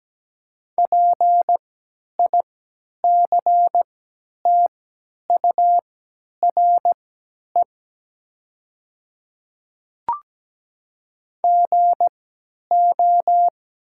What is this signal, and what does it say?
Morse code, heard as a keyed tone, about 17 wpm, spaced as for 10 wpm, 700 Hz. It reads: PICTURE GO
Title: 3x Character Spacing - Top 200 Words - 17wpm